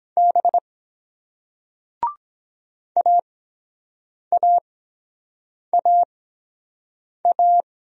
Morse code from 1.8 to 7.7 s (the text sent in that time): AAAA